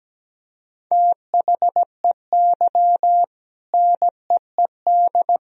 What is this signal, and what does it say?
Morse code, heard as a keyed tone, about 17 wpm, 700 Hz. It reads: THEY NEED
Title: Sentences from Top 1000 Words - Review Part 4 - 17wpm